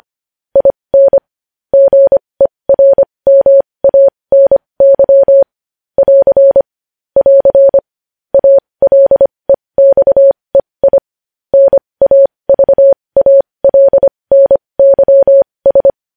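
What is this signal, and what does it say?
Morse code, heard as a keyed tone, about 25 wpm, spaced as for 20 wpm, 550 Hz. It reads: IN GERMANY <AR> <AR> ALEXEI NAVALNYS
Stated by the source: Morse news headlines practice at 20 wpm.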